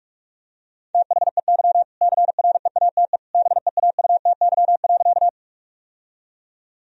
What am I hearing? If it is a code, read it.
Morse code, heard as a keyed tone, about 46 wpm, 700 Hz. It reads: THEY CREATE BEAUTY.
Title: Sentences from Top 1000 Words - Review Part 3 - 45wpm